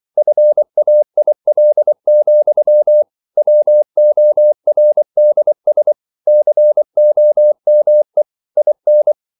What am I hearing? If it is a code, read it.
Morse code, heard as a keyed tone, about 24 wpm, 600 Hz. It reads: FAIL, WORDS COME IN
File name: CW / Practice / EnglishQuotations0324